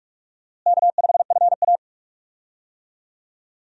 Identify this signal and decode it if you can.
Morse code, heard as a keyed tone, about 46 wpm, 700 Hz. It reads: K5FA
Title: USA Call Signs - 4 Characters - Repeated 2x - 45wpm